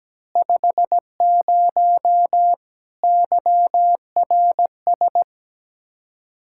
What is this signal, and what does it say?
Morse code, heard as a keyed tone, about 17 wpm, 700 Hz. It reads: 50 YRS
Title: Practice QSO - Exchange 3 - More Info - 17wpm